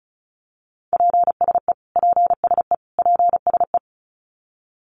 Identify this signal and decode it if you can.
Morse code, heard as a keyed tone, about 34 wpm, 700 Hz. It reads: PSE PSE PSE
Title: Abbreviations Repeated 3x - 35wpm